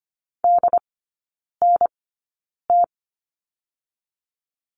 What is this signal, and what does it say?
Morse code, heard as a keyed tone, about 25 wpm, spaced as for 9 wpm, 700 Hz. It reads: DNT
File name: course / 25:10wpm / 093 Lesson 16e - Letter c - Three Characters 25wpm